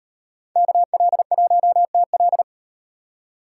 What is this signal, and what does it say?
Morse code, heard as a keyed tone, about 36 wpm, 700 Hz. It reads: KL1TL